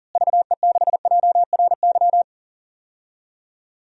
Morse code, heard as a keyed tone, about 40 wpm, 700 Hz. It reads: VE6JRY
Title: DXCC-100 Call Signs - 6 Characters - Repeated 3x - 40wpm